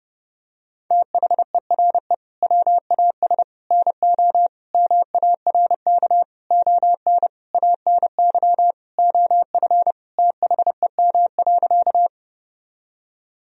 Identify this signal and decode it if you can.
Morse code, heard as a keyed tone, about 30 wpm, 700 Hz. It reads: THERE WAS NO MARK ON ANY OF THEM.
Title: Sentences from Top 1000 Words - Review Part 1 - 30wpm